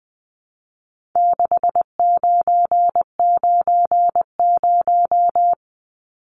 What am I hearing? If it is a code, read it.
Morse code, heard as a keyed tone, about 20 wpm, 700 Hz. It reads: 6990
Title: Numbers - 4-Digit 20wpm